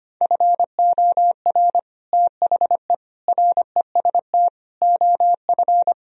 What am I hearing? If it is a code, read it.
Morse code, heard as a keyed tone, about 32 wpm, 700 Hz. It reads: FOR THE REST OF